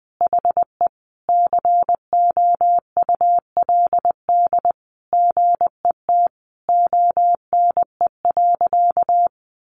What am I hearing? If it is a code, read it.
Morse code, heard as a keyed tone, about 20 wpm, 700 Hz. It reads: HE COULD GET ONE.
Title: Sentences from Top 1400 Words 20wpm